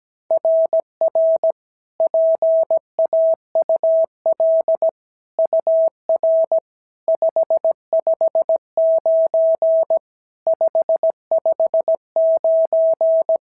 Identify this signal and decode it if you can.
Morse code, heard as a keyed tone, about 16 wpm, 650 Hz. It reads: RR PAUL UR 559 559
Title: SOTA QSO Practice 17wpm